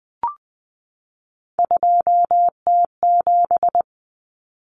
Morse code, heard as a keyed tone, about 20 wpm, 700 Hz. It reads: 2T7
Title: Common Cut Numbers - 3-Digit 20wpm